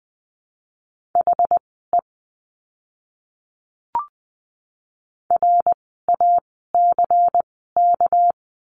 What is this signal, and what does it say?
Morse code, heard as a keyed tone, about 20 wpm, spaced as for 15 wpm, 700 Hz. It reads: HE RACK
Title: 199 Lesson 32f - Letter k - Words 20wpm